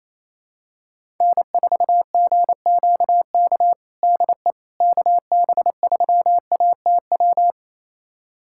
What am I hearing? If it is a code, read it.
Morse code, heard as a keyed tone, about 28 wpm, 700 Hz. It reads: N4GQK DE KB3ATW
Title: USA Callsign DE Callsign 28wpm